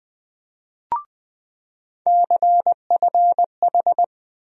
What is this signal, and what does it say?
Morse code, heard as a keyed tone, about 20 wpm, 700 Hz. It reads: CFH
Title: USA Call Sign Postfixes - 3 Letters 20wpm